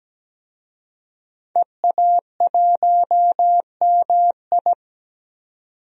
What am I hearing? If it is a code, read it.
Morse code, heard as a keyed tone, about 17 wpm, 700 Hz. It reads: EA1MI
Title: DXCC-100 Call Signs - 5 Characters - 17wpm